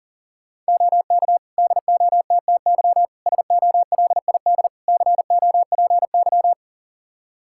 Morse code, heard as a keyed tone, about 40 wpm, 700 Hz. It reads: OK DOTTY SOLID COPY
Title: K1USN SST QSO Practice 40wpm